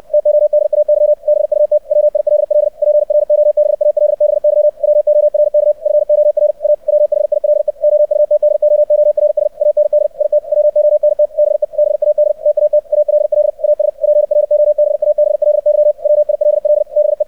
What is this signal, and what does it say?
Morse code, heard as a keyed tone, about 64 wpm, 600 Hz. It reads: TODAY BUT WILL PROBALBY WORK FOR A WHILE OUTDOORS AND IT WONT BE BAD TAT ALL AS PROBABLY WILL BE
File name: archived / CRSnet